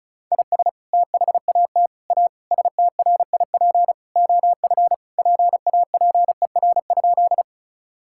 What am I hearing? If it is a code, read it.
Morse code, heard as a keyed tone, about 34 wpm, 700 Hz. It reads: IS THAT A STRIP OF PAPER?
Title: Sentences from Top 1600 Words 35wpm